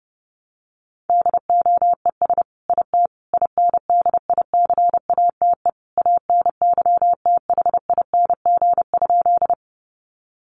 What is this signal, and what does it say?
Morse code, heard as a keyed tone, about 30 wpm, 700 Hz. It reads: DOES IT INDICATE ANYTHING?
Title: Sentences from Top 1300 Words 30wpm